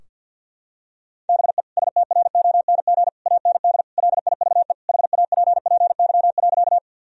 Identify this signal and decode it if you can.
Morse code, heard as a keyed tone, about 50 wpm, 700 Hz. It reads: BE STRONG AND LIVE HAPPY.
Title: Sentences from Top 1000 Words - Review Part 3 - 50wpm